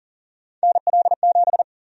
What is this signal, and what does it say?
Morse code, heard as a keyed tone, about 38 wpm, 700 Hz. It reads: NL7
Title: USA Call Sign Prefixes - Repeated 2x - 40wpm